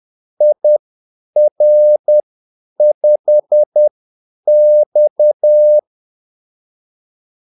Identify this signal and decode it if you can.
Morse code, heard as a keyed tone, about 10 wpm, spaced as for 8 wpm, 600 Hz. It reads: IR5X
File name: Callsigns-10-8wpm